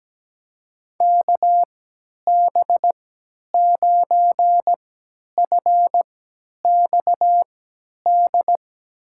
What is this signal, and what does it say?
Morse code, heard as a keyed tone, about 17 wpm, spaced as for 10 wpm, 700 Hz. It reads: KB9FXD
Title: USA Call Signs - 6 Characters - 3x Character Spacing - 17wpm